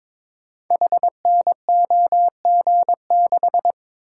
Morse code, heard as a keyed tone, about 22 wpm, 700 Hz. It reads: HNOG6
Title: Code Groups - 5 Letter:Numbers - 22wpm